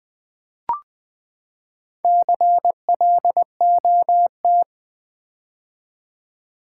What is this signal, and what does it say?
Morse code, heard as a keyed tone, about 20 wpm, 700 Hz. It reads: CLOT
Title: Build A Word - 4 Letters At A Time - 8-Letter Words - 20wpm